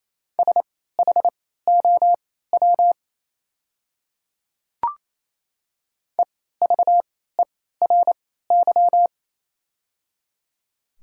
Morse code, heard as a keyed tone, about 27 wpm, spaced as for 16 wpm, 700 Hz. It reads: SHOW EVERY
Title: 3x Character Spacing - Top 300 Words - 28wpm